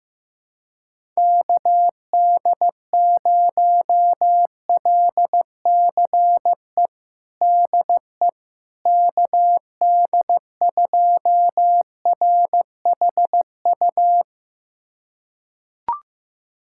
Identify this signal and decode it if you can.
Morse code, heard as a keyed tone, about 15 wpm, 700 Hz. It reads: KD0LCE DE KD2RHU
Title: USA Callsign DE Callsign - 6 Characters - 15wpm